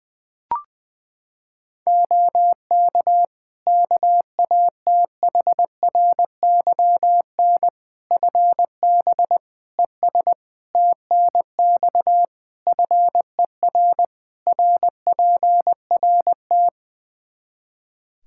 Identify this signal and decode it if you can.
Morse code, heard as a keyed tone, about 20 wpm, 700 Hz. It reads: OK KATHRYN FB ES TNX FER RPRT
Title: K1USN SST QSO Practice 20wpm